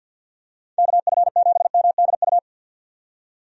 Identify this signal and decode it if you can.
Morse code, heard as a keyed tone, about 46 wpm, 700 Hz. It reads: KF7MDU
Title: USA Call Signs - 6 Characters - 50wpm